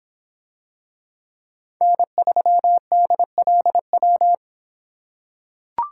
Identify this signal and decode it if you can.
Morse code, heard as a keyed tone, about 25 wpm, 700 Hz. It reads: N3DLW